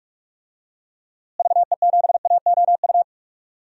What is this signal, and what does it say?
Morse code, heard as a keyed tone, about 46 wpm, 700 Hz. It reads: VE7AOU
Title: DXCC-100 Call Signs - 6 Characters - Repeated 2x - 45wpm